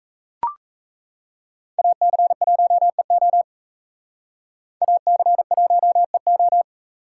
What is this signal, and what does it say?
Morse code, heard as a keyed tone, about 38 wpm, 700 Hz. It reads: AC1EO AC1EO